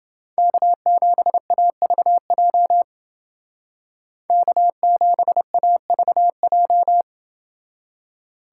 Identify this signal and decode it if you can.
Morse code, heard as a keyed tone, about 29 wpm, 700 Hz. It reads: K7AVJ K7AVJ